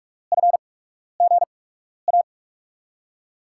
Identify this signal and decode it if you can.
Morse code, heard as a keyed tone, about 44 wpm, spaced as for 12 wpm, 700 Hz. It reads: FGA
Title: USA Call Signs - 8x Character Spacing - 45wpm